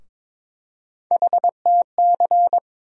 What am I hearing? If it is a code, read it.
Morse code, heard as a keyed tone, about 23 wpm, 700 Hz. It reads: HTC
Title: USA Call Sign Postfixes - 3 Letters 22wpm